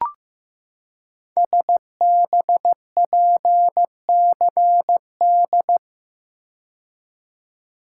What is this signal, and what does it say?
Morse code, heard as a keyed tone, about 15 wpm, 700 Hz. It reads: SBPCD
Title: Code Groups - 5 Letters - 15wpm